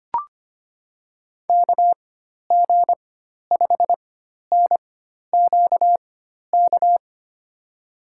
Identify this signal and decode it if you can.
Morse code, heard as a keyed tone, about 25 wpm, spaced as for 12 wpm, 700 Hz. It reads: KG5NQK
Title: USA Call Signs - 6 Characters - 4x Character Spacing - 25wpm